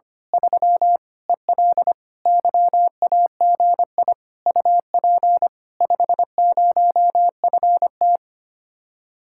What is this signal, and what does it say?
Morse code, heard as a keyed tone, about 25 wpm, 700 Hz. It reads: 3 EL YAGI UP 50FT
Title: Speed-Racing Common QSO Phrases - 25wpm